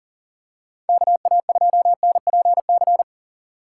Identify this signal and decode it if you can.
Morse code, heard as a keyed tone, about 40 wpm, 700 Hz. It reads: KA2NPC
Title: USA Call Signs - 6 Characters - 6x Character Spacing - 40wpm